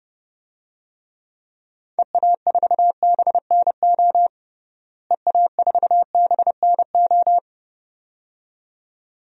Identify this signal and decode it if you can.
Morse code, heard as a keyed tone, about 30 wpm, 700 Hz. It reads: EA4BNO EA4BNO
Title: DXCC-100 Call Signs - 6 Characters - Repeated 2x - 30wpm